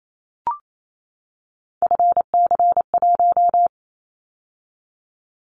Morse code, heard as a keyed tone, about 28 wpm, 700 Hz. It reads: FC1
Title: Callsign Prefixes for Top 50 DXCC - 28wpm